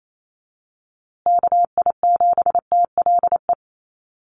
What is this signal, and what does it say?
Morse code, heard as a keyed tone, about 28 wpm, 700 Hz. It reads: KI7TLE
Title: USA Call Signs Repeated 2x 28wpm